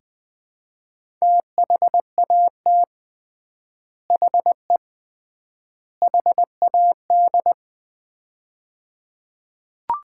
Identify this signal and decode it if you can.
Morse code, heard as a keyed tone, about 21 wpm, 700 Hz. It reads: THAT HE HAD
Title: 3x Word-Spacing - Sets of 3 Words - 20wpm